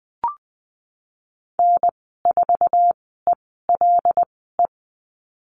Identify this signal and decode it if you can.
Morse code, heard as a keyed tone, about 20 wpm, spaced as for 15 wpm, 700 Hz. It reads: N4ELE
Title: 088 Lesson 15g - Number 5 - Call Signs 20wpm